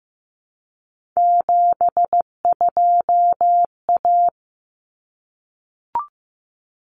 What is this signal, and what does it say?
Morse code, heard as a keyed tone, about 15 wpm, 700 Hz. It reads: 72A
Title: Common Cut Numbers - 3-Digit 15wpm